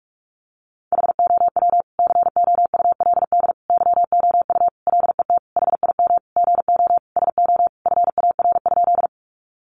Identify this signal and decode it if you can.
Morse code, heard as a keyed tone, about 45 wpm, 700 Hz. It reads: HOW COULD YOU LET HIM GO SO FAR?
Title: Sentences from Top 300 Words 45wpm